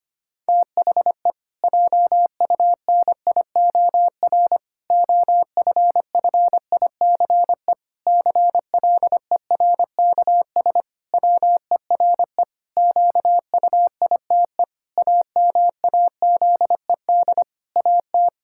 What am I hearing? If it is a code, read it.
Morse code, heard as a keyed tone, about 25 wpm, 700 Hz. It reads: THE JUNIOR OFFICE CLERKS WERE QUITE AMAZED AT
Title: Pangram Challenge v2 - 25wpm